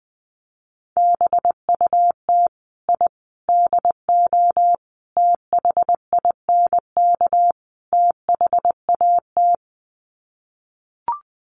Morse code, heard as a keyed tone, about 20 wpm, 700 Hz. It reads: BUT I DO THINK THAT
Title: Sets of 5 Words Encore2 20wpm